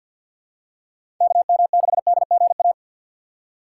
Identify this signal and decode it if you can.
Morse code, heard as a keyed tone, about 48 wpm, 700 Hz. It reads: KM6DGA
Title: USA Call Signs - 6x Character Spacing - 50wpm